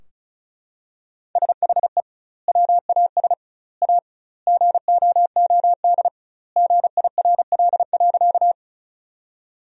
Morse code, heard as a keyed tone, about 36 wpm, 700 Hz. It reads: SHE WAS A GOOD GIRL.
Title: 2x Word-Spacing - Sentences from Top 400 Words - 35wpm